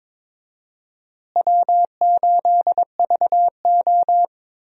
Morse code, heard as a keyed tone, about 21 wpm, 700 Hz. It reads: W8VO
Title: USA Call Signs - 4 Characters - Repeated 2x - 22wpm